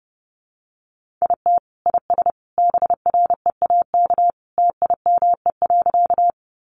Morse code, heard as a keyed tone, about 30 wpm, 700 Hz. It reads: IT IS BREAK TIME.
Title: Easy Phrases 30wpm